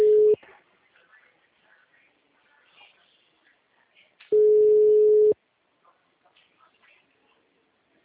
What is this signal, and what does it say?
Ringback tone (Europe, 425 Hz).
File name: out- -2025